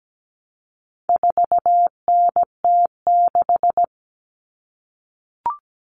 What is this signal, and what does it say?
Morse code, heard as a keyed tone, about 17 wpm, 700 Hz. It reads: 4NT6
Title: All Cut Numbers - 4-Digit 17wpm